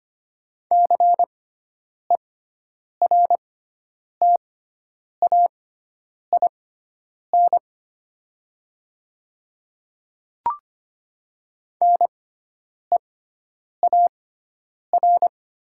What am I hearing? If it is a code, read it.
Morse code, heard as a keyed tone, about 25 wpm, spaced as for 9 wpm, 700 Hz. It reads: CERTAIN NEAR
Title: 6x Character Spacing - Top 500 Words - 25wpm